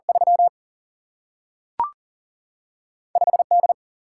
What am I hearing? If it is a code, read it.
Morse code, heard as a keyed tone, about 39 wpm, 700 Hz. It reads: 3 5D
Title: Single-Letter Single-Number 40wpm